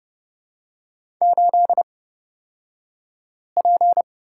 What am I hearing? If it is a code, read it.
Morse code, heard as a keyed tone, about 30 wpm, spaced as for 5 wpm, 700 Hz. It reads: 8P
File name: Lesson 38d - Letter q - Two Characters 30wpm